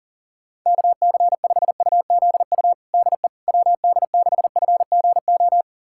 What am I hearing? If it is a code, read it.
Morse code, heard as a keyed tone, about 40 wpm, 700 Hz. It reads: KC5UZU DE WD6FGO
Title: USA Callsign DE Callsign 40wpm